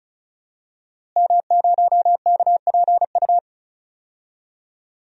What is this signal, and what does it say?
Morse code, heard as a keyed tone, about 34 wpm, 700 Hz. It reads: M0KPU
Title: DXCC-100 Call Signs - 5 Characters - Repeated 3x - 35wpm